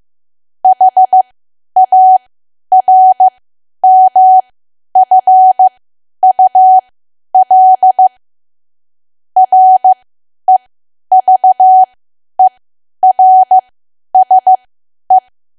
Morse code, heard as a keyed tone, about 15 wpm, spaced as for 10 wpm, 750 Hz. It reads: HARMFUL REVERSE
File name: cwa / W1AW-Level2 / 10WPM